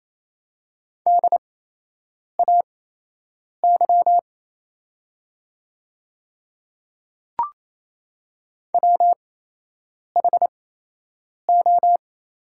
Morse code, heard as a keyed tone, about 28 wpm, spaced as for 8 wpm, 700 Hz. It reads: DAY WHO
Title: 8x Character Spacing - Top 100 Words - 28wpm